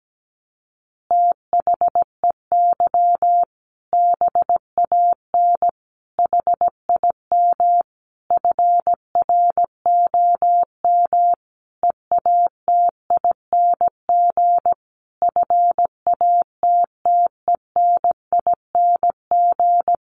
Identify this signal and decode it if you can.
Morse code, heard as a keyed tone, about 17 wpm, 700 Hz. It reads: THEY BAN HIM FROM EATING FATTENING
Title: Easy Phrases 17wpm